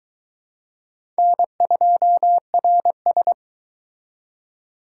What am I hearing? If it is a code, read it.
Morse code, heard as a keyed tone, about 22 wpm, 700 Hz. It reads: N2RS